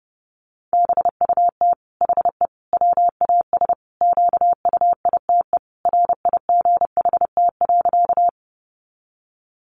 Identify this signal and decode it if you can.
Morse code, heard as a keyed tone, about 30 wpm, 700 Hz. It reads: BUT HE WAS QUITE RIGHT.
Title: Sentences from Top 700 Words 30wpm